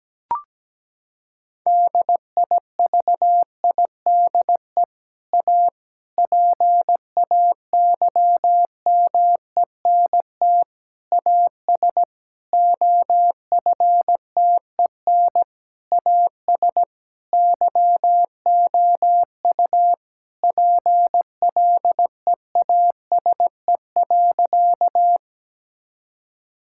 Morse code, heard as a keyed tone, about 17 wpm, 700 Hz. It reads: DIVIDE A PAYMENT AS OFTEN AS YOU PLEASE.
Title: Sentences from Top 1600 Words 17wpm